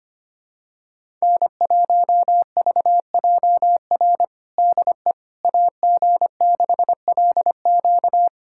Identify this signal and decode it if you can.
Morse code, heard as a keyed tone, about 25 wpm, 700 Hz. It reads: N1VJR DE AG6LQ
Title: USA Callsign DE Callsign - 5 Characters - 25wpm